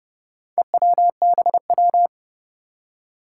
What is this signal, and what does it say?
Morse code, heard as a keyed tone, about 30 wpm, 700 Hz. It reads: EWBW
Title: Four Challenging Letter-Number Combinations 30wpm